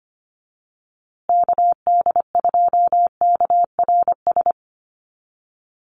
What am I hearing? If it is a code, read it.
Morse code, heard as a keyed tone, about 25 wpm, 700 Hz. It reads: KD2KRS
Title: USA Call Signs Repeated 2x 25wpm